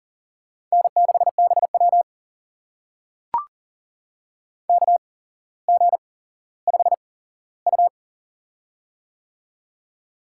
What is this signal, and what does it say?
Morse code, heard as a keyed tone, about 40 wpm, spaced as for 17 wpm, 700 Hz. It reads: N6BW KG5U
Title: USA Call Signs - 4 Characters - 8x Character Spacing - 40wpm